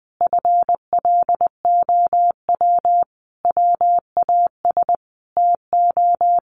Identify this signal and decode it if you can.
Morse code, heard as a keyed tone, about 20 wpm, 700 Hz. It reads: FLOW WAS TO
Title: Sentences from Top 1200 Words 20wpm